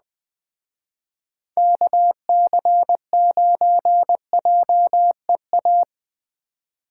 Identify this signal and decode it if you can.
Morse code, heard as a keyed tone, about 21 wpm, 700 Hz. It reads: KC9JEA
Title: Speed Racing - USA 2x3 Call Signs 17wpm